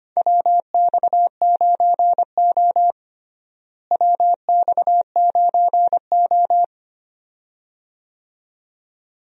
Morse code, heard as a keyed tone, about 25 wpm, 700 Hz. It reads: WX9O WX9O
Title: USA Call Signs - 4 Characters - Repeated 2x - 25wpm